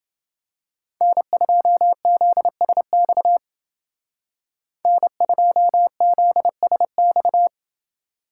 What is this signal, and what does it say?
Morse code, heard as a keyed tone, about 28 wpm, 700 Hz. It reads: N2ZSX N2ZSX